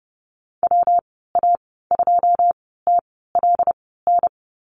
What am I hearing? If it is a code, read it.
Morse code, heard as a keyed tone, about 31 wpm, spaced as for 17 wpm, 700 Hz. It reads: WA2TLN
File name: Lesson 19g - Letter w - Call Signs 30wpm